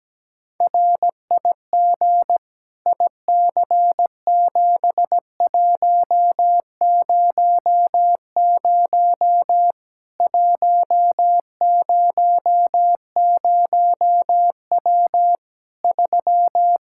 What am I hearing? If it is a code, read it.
Morse code, heard as a keyed tone, about 17 wpm, 700 Hz. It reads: RIG IC7100 100W 3
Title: Practice QSO - Exchange 2 - Info - 17wpm